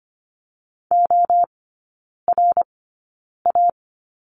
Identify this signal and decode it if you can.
Morse code, heard as a keyed tone, about 25 wpm, spaced as for 9 wpm, 700 Hz. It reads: ORA
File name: course / 25:10wpm / 100 Lesson 17e - Letter u - Three Characters 25wpm